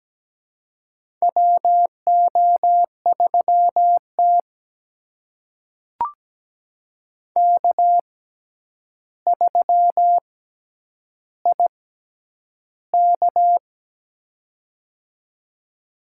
Morse code, heard as a keyed tone, about 17 wpm, spaced as for 9 wpm, 700 Hz. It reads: WO3T K3IK
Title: USA Call Signs - 4 Characters - 6x Character Spacing - 17wpm